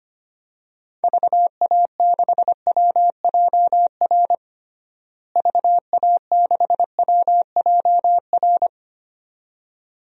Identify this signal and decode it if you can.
Morse code, heard as a keyed tone, about 25 wpm, 700 Hz. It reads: VA6WJR VA6WJR
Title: DXCC-100 Call Signs - 6 Characters - Repeated 2x - 25wpm